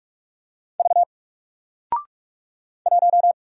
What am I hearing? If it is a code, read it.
Morse code, heard as a keyed tone, about 46 wpm, 700 Hz. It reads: V 1
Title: Single Character (Letter:Number:Punctuation:Prosign) Alt 45wpm